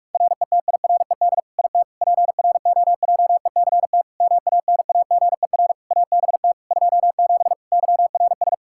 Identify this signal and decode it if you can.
Morse code, heard as a keyed tone, about 46 wpm, 700 Hz. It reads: RETIRED IT PROJECT MANAGER ABT 27 YRS
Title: Practice QSO - Exchange 3 - More Info - 45wpm